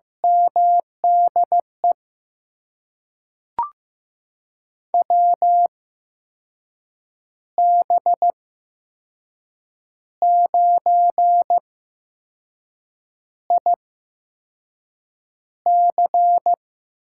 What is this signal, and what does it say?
Morse code, heard as a keyed tone, about 15 wpm, spaced as for 5 wpm, 700 Hz. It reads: MDE WB9IC
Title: USA Call Signs - 6 Characters - 8x Character Spacing - 15wpm